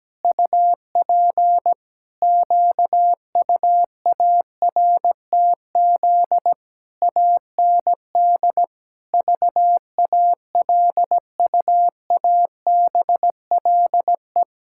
Morse code, heard as a keyed tone, about 17 wpm, 700 Hz. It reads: UP QUARTZ AND VALUABLE
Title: Pangram Challenge v2 - 17wpm